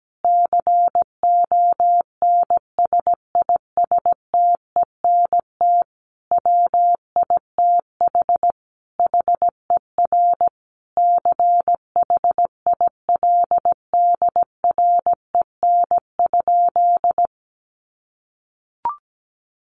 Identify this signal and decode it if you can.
Morse code, heard as a keyed tone, about 17 wpm, 700 Hz. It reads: CONSISTENT WITH HER CHILDREN?
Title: Sentences from Top 1500 Words 17wpm